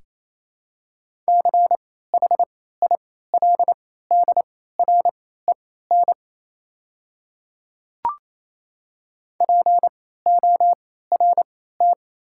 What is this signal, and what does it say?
Morse code, heard as a keyed tone, about 28 wpm, spaced as for 16 wpm, 700 Hz. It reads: CHILDREN PORT
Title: 3x Character Spacing - Top 300 Words - 28wpm